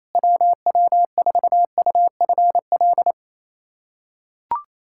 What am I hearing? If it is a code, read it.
Morse code, heard as a keyed tone, about 28 wpm, 700 Hz. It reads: WW4UFL
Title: USA Callsign DE Callsign - 6 Characters - 28wpm